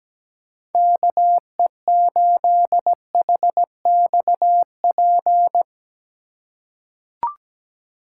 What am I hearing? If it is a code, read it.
Morse code, heard as a keyed tone, about 17 wpm, 700 Hz. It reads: KE8HXP
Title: USA Call Signs - 3x Character Spacing - 17wpm